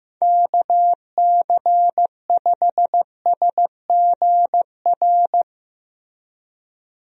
Morse code, heard as a keyed tone, about 15 wpm, 700 Hz. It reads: KC5SGR
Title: USA Call Signs - 6 Characters - Repeated 3x - 15wpm